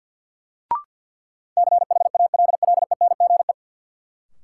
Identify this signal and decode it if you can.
Morse code, heard as a keyed tone, about 50 wpm, 700 Hz. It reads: CHALLENGE
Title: Sentences from Top 1300 Words - New Word Drill Alt - 50wpm